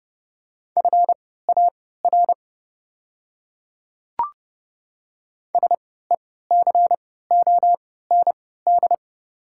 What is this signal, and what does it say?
Morse code, heard as a keyed tone, about 32 wpm, spaced as for 17 wpm, 700 Hz. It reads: FAR SECOND
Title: 3x Character Spacing - Top 300 Words - 30wpm